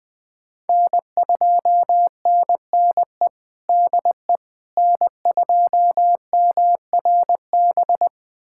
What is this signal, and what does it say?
Morse code, heard as a keyed tone, about 21 wpm, 700 Hz. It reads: N2NNE DE N2MRB
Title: USA Callsign DE Callsign - 5 Characters - 20wpm